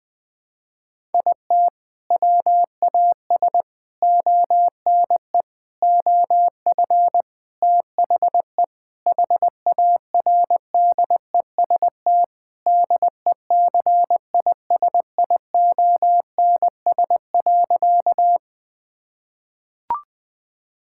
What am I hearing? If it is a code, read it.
Morse code, heard as a keyed tone, about 20 wpm, 700 Hz. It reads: IT WAS ONE OF THE HARDEST DECISIONS.
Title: Sentences with 5-Word Phrases - Spoken Beginning - 20wpm